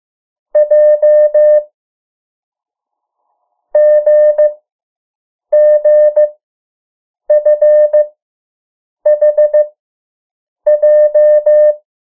Morse code, heard as a keyed tone, about 15 wpm, 600 Hz. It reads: J G G F H J
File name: morse mp3 / 03 urok 2